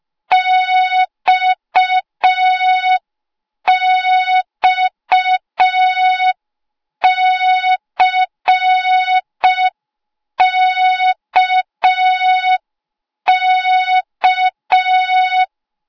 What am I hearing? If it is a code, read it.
Morse code, heard as a keyed tone, about 5 wpm, 3700 Hz. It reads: XXCKK